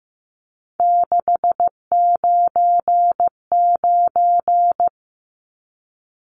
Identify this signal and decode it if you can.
Morse code, heard as a keyed tone, about 15 wpm, 700 Hz. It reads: 699
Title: Numbers - 3-Digit 15wpm